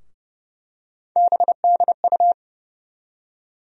Morse code, heard as a keyed tone, about 30 wpm, 700 Hz. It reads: BDU
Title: USA Call Sign Postfixes - 3 Letters 30wpm